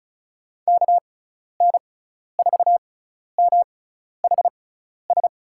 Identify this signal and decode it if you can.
Morse code, heard as a keyed tone, about 36 wpm, spaced as for 12 wpm, 700 Hz. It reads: KN4MHS
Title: USA Call Signs - 6 Characters - 6x Character Spacing - 35wpm